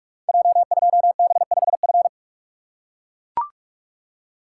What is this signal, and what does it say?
Morse code, heard as a keyed tone, about 44 wpm, 700 Hz. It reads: J2B5F
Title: Code Groups - 5 Letter:Numbers - 45wpm